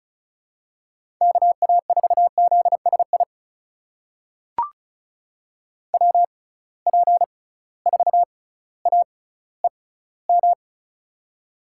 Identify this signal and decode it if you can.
Morse code, heard as a keyed tone, about 36 wpm, spaced as for 18 wpm, 700 Hz. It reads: KA4ZSI WP4AEM
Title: USA Call Signs - 6 Characters - 6x Character Spacing - 35wpm